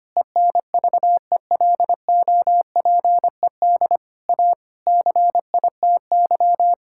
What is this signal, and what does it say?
Morse code, heard as a keyed tone, about 25 wpm, 700 Hz. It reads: ENVELOPED A CITY
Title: Pangram Challenge v2 - 25wpm